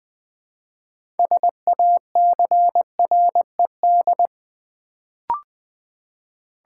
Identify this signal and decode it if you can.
Morse code, heard as a keyed tone, about 20 wpm, 700 Hz. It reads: SACRED
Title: Build A Word - 3 Letters At A Time - 6-Letter Words - 20wpm